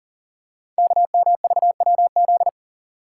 Morse code, heard as a keyed tone, about 42 wpm, 700 Hz. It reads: KMVWZ
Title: Code Groups - 5 Letters - 40wpm